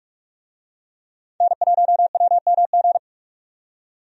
Morse code, heard as a keyed tone, about 41 wpm, 700 Hz. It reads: N1WMG
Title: USA Call Signs - 5 Characters - 45wpm